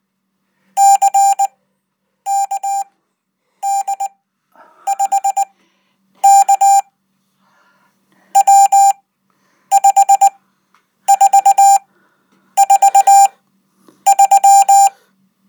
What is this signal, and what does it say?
Morse code, heard as a keyed tone, about 19 wpm, spaced as for 8 wpm, 790 Hz. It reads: CKD5K W5443
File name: Code